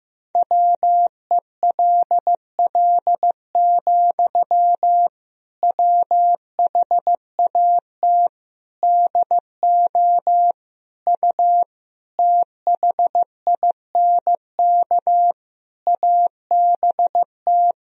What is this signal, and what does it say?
Morse code, heard as a keyed tone, about 15 wpm, 700 Hz. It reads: WELL, WHAT DO U THINK ABT
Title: Sentences from Top 200 Words and Abbreviations 15wpm